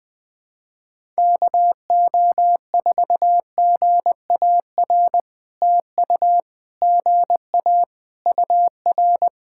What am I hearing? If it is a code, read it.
Morse code, heard as a keyed tone, about 17 wpm, 700 Hz. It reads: KO4GAR TU GA UR
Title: POTA QSO Practice - 20wpm